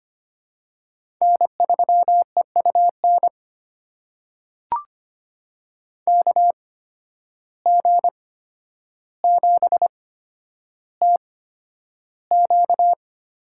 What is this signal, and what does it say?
Morse code, heard as a keyed tone, about 25 wpm, spaced as for 11 wpm, 700 Hz. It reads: N3EUN KG7TQ
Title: USA Call Signs - 8x Character Spacing - 25wpm